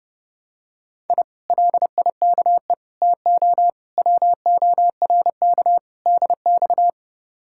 Morse code, heard as a keyed tone, about 29 wpm, 700 Hz. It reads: I LIKE TO WORK DX